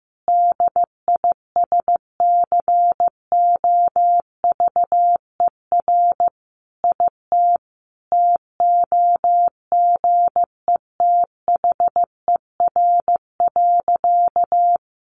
Morse code, heard as a keyed tone, about 15 wpm, 700 Hz. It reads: DISCOVER IT TOGETHER.
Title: Sentences from Top 1100 Words 15wpm